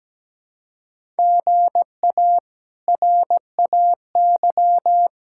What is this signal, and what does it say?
Morse code, heard as a keyed tone, about 17 wpm, 700 Hz. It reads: GA RAY
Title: K1USN SST QSO Practice - 17wpm